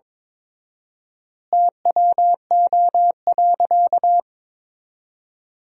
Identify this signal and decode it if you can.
Morse code, heard as a keyed tone, about 22 wpm, 700 Hz. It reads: TWO.
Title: 4x Word-Spacing - Sentences from Top 200 Words - 22wpm